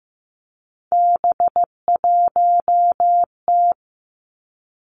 Morse code, heard as a keyed tone, about 15 wpm, 700 Hz. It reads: B1T
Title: Three Challenging Letter-Number Combinations 15wpm